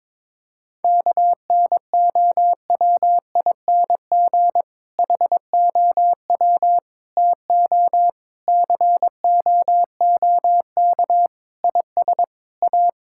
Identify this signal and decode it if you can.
Morse code, heard as a keyed tone, about 22 wpm, 700 Hz. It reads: KNOWING HOW TO COOK IS A
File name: Difficult Sentences with Hints - 22wpm